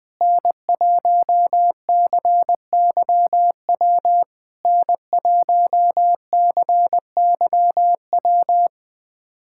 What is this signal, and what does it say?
Morse code, heard as a keyed tone, about 20 wpm, 700 Hz. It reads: N1CYW N1CYW
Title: SOTA QSO Practice 20wpm